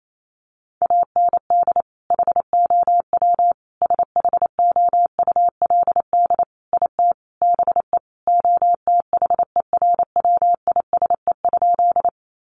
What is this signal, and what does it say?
Morse code, heard as a keyed tone, about 28 wpm, 700 Hz. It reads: AND HOW SHOULD IT BE OTHERWISE?
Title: Sentences from Top 1200 Words 28wpm